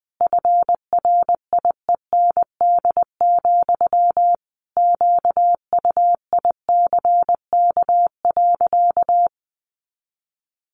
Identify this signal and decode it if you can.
Morse code, heard as a keyed tone, about 20 wpm, 700 Hz. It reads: FRIEND, QUICK.
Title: Sentences from Top 500 Words 20wpm